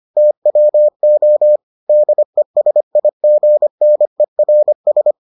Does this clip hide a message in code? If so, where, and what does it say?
Yes: Morse code from 0.2 to 5.1 s, heard as a keyed tone, about 25 wpm, 600 Hz. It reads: TWO DESIGNERS